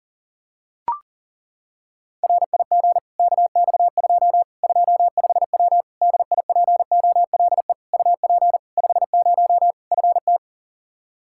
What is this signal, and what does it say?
Morse code, heard as a keyed tone, about 40 wpm, 700 Hz. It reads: RIG KX2 25W DIPOLE UP 50 FT
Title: Practice QSO - Exchange 2 - Info - 40wpm